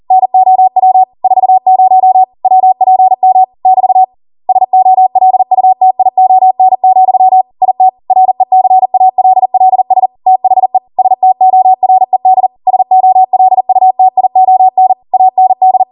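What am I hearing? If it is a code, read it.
Morse code, heard as a keyed tone, about 40 wpm, 750 Hz. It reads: NOW 40 WPM <BT> SOLUTION, IT RECALLS THE STORED SOLUTION AND
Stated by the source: Code practice at 40 wpm.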